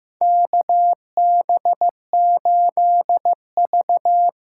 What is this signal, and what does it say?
Morse code, heard as a keyed tone, about 15 wpm, 700 Hz. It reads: KB8V
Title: USA Call Signs - 6 Characters - 8x Character Spacing - 15wpm